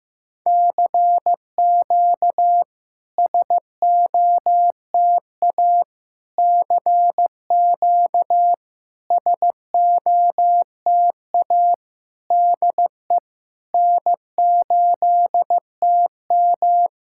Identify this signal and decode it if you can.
Morse code, heard as a keyed tone, about 15 wpm, 690 Hz. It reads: CQ SOTA CQ SOTA DE N8TM
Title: SOTA QSO Practice 15wpm